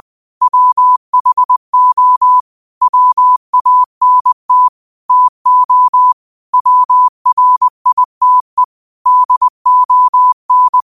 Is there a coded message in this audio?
Yes: Morse code from 0.4 to 10.8 s, heard as a keyed tone, about 20 wpm, 1000 Hz. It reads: WHO WANT TO WRITE DON